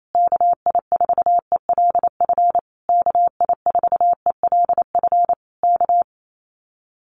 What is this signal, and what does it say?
Morse code, heard as a keyed tone, about 28 wpm, 700 Hz. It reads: KI4ELF KI4ELF K
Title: Common QSO Phrases Encore 28wpm